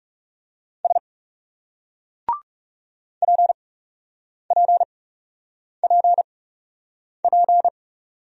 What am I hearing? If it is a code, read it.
Morse code, heard as a keyed tone, about 36 wpm, spaced as for 8 wpm, 700 Hz. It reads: S PPPP
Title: Speed-Racing Single Letters 30wpm